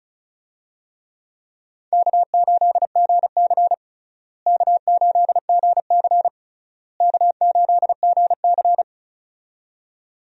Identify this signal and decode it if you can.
Morse code, heard as a keyed tone, about 36 wpm, 700 Hz. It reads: K8GC K8GC K8GC
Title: USA Call Signs - 4 Characters - Repeated 3x - 35wpm